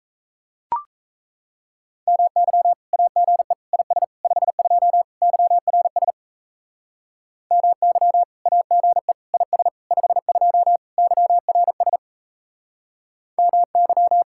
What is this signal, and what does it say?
Morse code, heard as a keyed tone, about 36 wpm, 700 Hz. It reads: MY AGE IS 52 YRS MY AGE IS 52 YRS MY